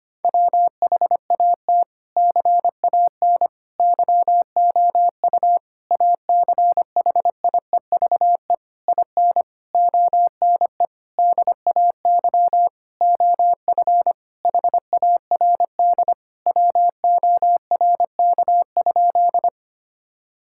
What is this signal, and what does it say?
Morse code, heard as a keyed tone, about 25 wpm, 700 Hz. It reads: WHAT CAN YOU ACHIEVE IN ONE DAY OF HARD WORK?
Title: Sentences from Top 1000 Words - Review Part 6 - 25wpm